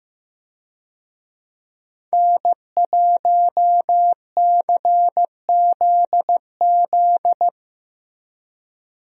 Morse code, heard as a keyed tone, about 15 wpm, 700 Hz. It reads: N1CZZ
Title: USA Call Signs - 5 Characters - Repeated 3x - 15wpm